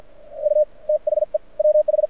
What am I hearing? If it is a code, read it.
Morse code, heard as a keyed tone, about 55 wpm, spaced as for 47 wpm, 600 Hz. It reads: U THE WH
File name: archived / CRSnet